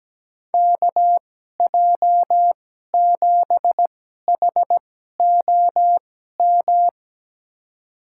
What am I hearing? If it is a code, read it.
Morse code, heard as a keyed tone, about 17 wpm, spaced as for 12 wpm, 700 Hz. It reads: KJ7HOM
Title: USA Call Signs - 2x Character Spacing - 17wpm